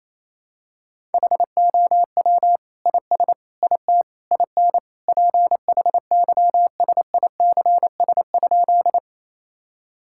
Morse code, heard as a keyed tone, about 28 wpm, 700 Hz. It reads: HOW IS IT IN PHYSICS?
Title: Sentences from Top 1000 Words - Review Part 5 - 28wpm